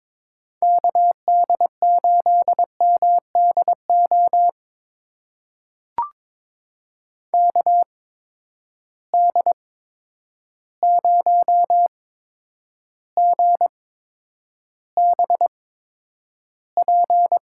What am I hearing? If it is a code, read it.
Morse code, heard as a keyed tone, about 22 wpm, spaced as for 9 wpm, 700 Hz. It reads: KD8MDO KD0GBP
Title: USA Call Signs - 6 Characters - 8x Character Spacing - 22wpm